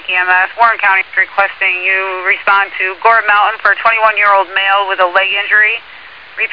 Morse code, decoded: none

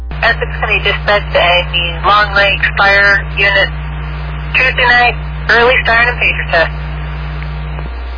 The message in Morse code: none

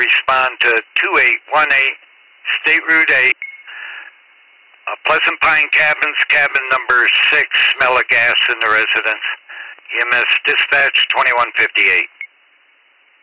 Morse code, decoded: none

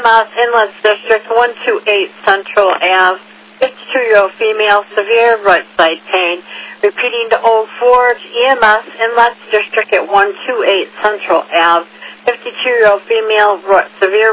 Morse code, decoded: none

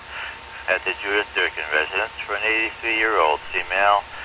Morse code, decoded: none

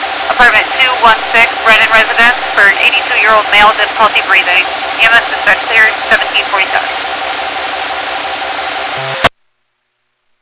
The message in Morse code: none

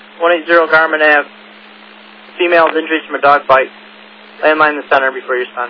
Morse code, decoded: none

